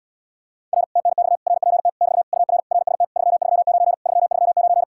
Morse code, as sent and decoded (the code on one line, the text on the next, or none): .-. / - -. -..- / .-. .--. - / -...- / ..- .-. / .-. ... - / ...-- ...-- ----. / ...-- ...-- ----.
R TNX RPT <BT> UR RST 339 339